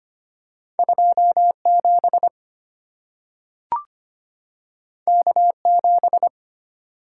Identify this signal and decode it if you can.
Morse code, heard as a keyed tone, about 25 wpm, 700 Hz. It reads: 27 K7
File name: Two Letter-Number Combinations - 25wpm